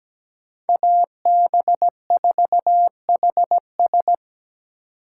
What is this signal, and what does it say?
Morse code, heard as a keyed tone, about 20 wpm, 700 Hz. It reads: AB4HS
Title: Speed Racing - USA 2x2 Call Signs 17wpm